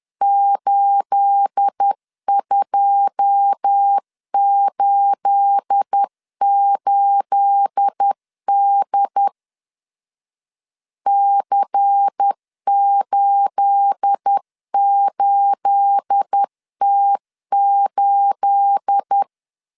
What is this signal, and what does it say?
Morse code, heard as a keyed tone, about 10 wpm, 800 Hz. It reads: 8288D C88T8
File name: LEZIONI / 9 28 10wpm 2842